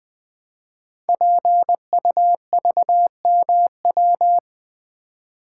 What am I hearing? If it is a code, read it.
Morse code, heard as a keyed tone, about 20 wpm, 700 Hz. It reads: PUVMW
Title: Code Groups - 5 Letters - 20wpm